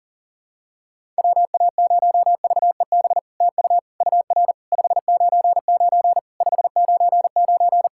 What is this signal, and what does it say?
Morse code, heard as a keyed tone, about 37 wpm, 690 Hz. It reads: WA0VEB TU UR 599 599
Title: SOTA QSO Practice 40wpm